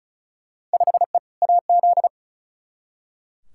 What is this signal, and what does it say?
Morse code, heard as a keyed tone, about 34 wpm, 700 Hz. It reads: 5E AZ
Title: Field Day Exchanges - 35wpm